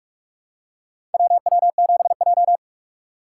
Morse code, heard as a keyed tone, about 41 wpm, 700 Hz. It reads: WW7J
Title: USA Call Signs - 4 Characters - 4x Character Spacing - 45wpm